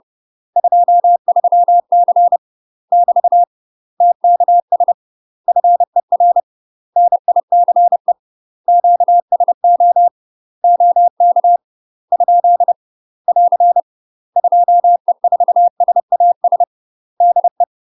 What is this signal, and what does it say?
Morse code, heard as a keyed tone, about 30 wpm, 700 Hz. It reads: 23C <BT> TKS FER NICE QSO OK ? <AR> 2E4SAS DE
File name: UK MASTER QSOs TRAINING / 30WPM / UK MASTER QSOs 0011